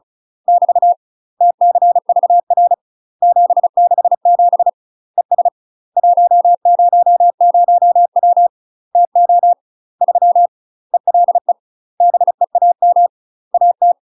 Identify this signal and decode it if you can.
Morse code, heard as a keyed tone, about 36 wpm, 700 Hz. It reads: <BT> TCVR 767 ES 100W TO 3 ELE BEAM AT
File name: UK MASTER QSOs TRAINING / 35WPM / UK MASTER QSOs 0017